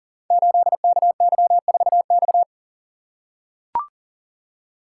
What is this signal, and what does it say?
Morse code, heard as a keyed tone, about 39 wpm, 690 Hz. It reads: 8KY4X
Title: Code Groups - 5 Letter:Numbers - 40wpm